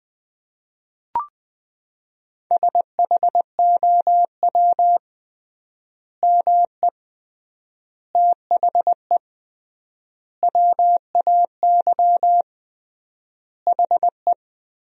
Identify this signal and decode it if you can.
Morse code, heard as a keyed tone, about 20 wpm, 700 Hz. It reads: SHOW ME THE WAY HE
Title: 3x Word-Spacing - Sentences from Top 200 Words - 20wpm